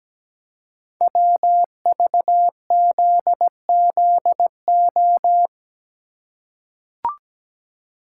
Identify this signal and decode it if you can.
Morse code, heard as a keyed tone, about 17 wpm, 700 Hz. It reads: WVZZO
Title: Code Groups - 5 Letters - 17wpm